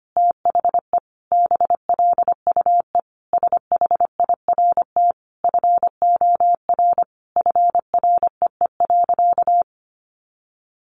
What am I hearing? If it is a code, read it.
Morse code, heard as a keyed tone, about 25 wpm, 700 Hz. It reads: THE BLUE SHIRT FOR FREE.
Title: Additional Easy Phrases 25wpm